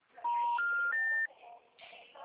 Special information tone.